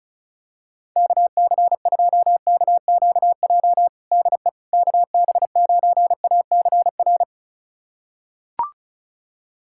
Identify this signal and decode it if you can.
Morse code, heard as a keyed tone, about 34 wpm, 700 Hz. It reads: KC2KQJ DE KB9ACR
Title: USA Callsign DE Callsign - 6 Characters - 35wpm